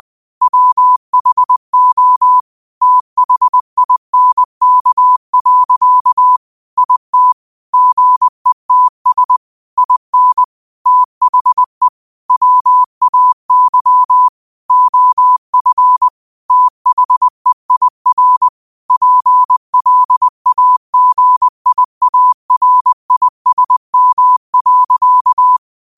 Morse code, heard as a keyed tone, about 20 wpm, 1000 Hz. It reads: WHO THINK. IT GETS IN THE WAY OF THEIR PLAGIARISM.
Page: Quotes for Tue, 12 Aug 2025 in Morse Code at 20 words per minute.